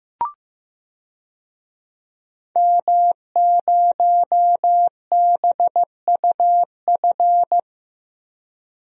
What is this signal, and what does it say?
Morse code, heard as a keyed tone, about 15 wpm, 700 Hz. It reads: M0BUF
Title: DXCC-100 Call Signs - 5 Characters - 15wpm